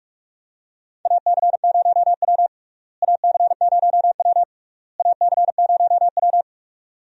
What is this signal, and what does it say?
Morse code, heard as a keyed tone, about 41 wpm, 700 Hz. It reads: AC0W AC0W AC0W
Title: USA Call Signs - 4 Characters - Repeated 3x - 45wpm